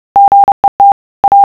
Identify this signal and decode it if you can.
Morse code, heard as a keyed tone, about 30 wpm, 800 Hz. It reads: GET A